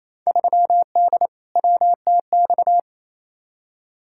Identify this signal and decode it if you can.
Morse code, heard as a keyed tone, about 28 wpm, 700 Hz. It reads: 3D WTX
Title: Field Day Exchanges Encore - 28wpm